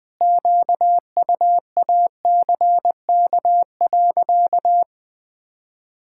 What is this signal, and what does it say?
Morse code, heard as a keyed tone, about 20 wpm, 700 Hz. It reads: QUACK.
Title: Pangram Challenge v2 - 20wpm